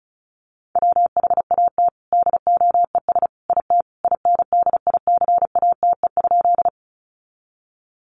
Morse code, heard as a keyed tone, about 34 wpm, 700 Hz. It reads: WHAT DOES IT INDICATE?
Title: Sentences from Top 1300 Words 35wpm